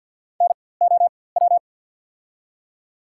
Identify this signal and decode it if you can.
Morse code, heard as a keyed tone, about 51 wpm, spaced as for 23 wpm, 700 Hz. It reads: NOW
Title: 4x Character Spacing - Top 100 Words - 50wpm